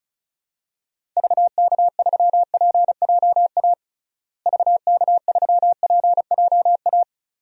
Morse code, heard as a keyed tone, about 34 wpm, 700 Hz. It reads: VK3PJA VK3PJA
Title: DXCC-100 Call Signs - 6 Characters - Repeated 2x - 35wpm